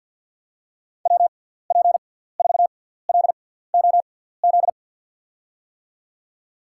Morse code, heard as a keyed tone, about 50 wpm, spaced as for 17 wpm, 700 Hz. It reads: WP4LOZ
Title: USA Call Signs - 6 Characters - 6x Character Spacing - 50wpm